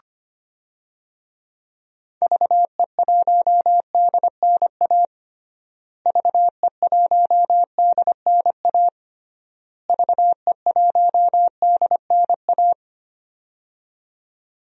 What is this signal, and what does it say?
Morse code, heard as a keyed tone, about 25 wpm, 700 Hz. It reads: VE1DNA VE1DNA VE1DNA
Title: DXCC-100 Call Signs - 6 Characters - Repeated 3x - 25wpm